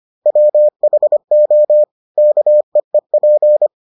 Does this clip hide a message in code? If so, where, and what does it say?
Yes: Morse code from 0.3 to 3.7 s, heard as a keyed tone, about 25 wpm, 600 Hz. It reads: WHO KEEP